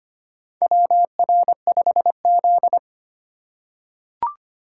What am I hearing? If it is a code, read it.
Morse code, heard as a keyed tone, about 25 wpm, 700 Hz. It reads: WR5Z
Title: USA Call Signs - 4 Characters - 3x Character Spacing - 25wpm